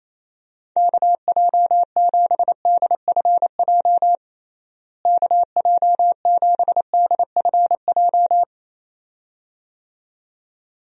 Morse code, heard as a keyed tone, about 28 wpm, 700 Hz. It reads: KJ7DFJ KJ7DFJ
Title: USA Call Signs - 6 Characters - Repeated 2x - 28wpm